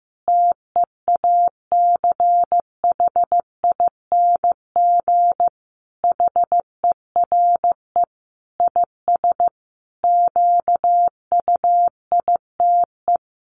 Morse code, heard as a keyed tone, about 15 wpm, 700 Hz. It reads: TEACHING HERE IS QUITE
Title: Sentences from Top 1100 Words 15wpm